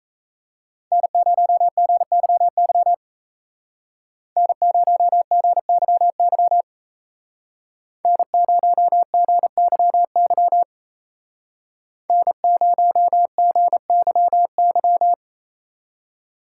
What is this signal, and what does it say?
Morse code, heard as a keyed tone, about 33 wpm, 700 Hz. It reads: N0GYY N0GYY N0GYY N0GYY
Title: Speed Racing - USA 1x3 Call Signs 28wpm